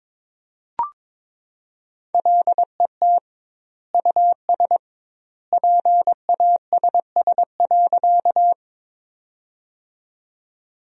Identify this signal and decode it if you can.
Morse code, heard as a keyed tone, about 21 wpm, 700 Hz. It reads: LET US PASS.
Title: 2x Word-Spacing - Sentences from Top 400 Words - 22wpm